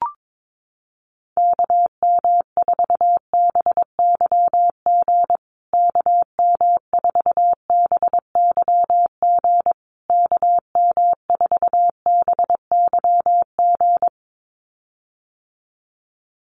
Morse code, heard as a keyed tone, about 22 wpm, 700 Hz. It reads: KM4BYG KM4BYG KM4BYG
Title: USA Call Signs Repeated 3x 22wpm